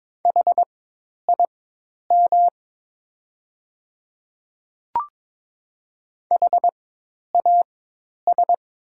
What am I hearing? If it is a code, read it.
Morse code, heard as a keyed tone, about 22 wpm, spaced as for 10 wpm, 700 Hz. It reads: HIM HAS
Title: 4x Character Spacing - Top 300 Words - 22wpm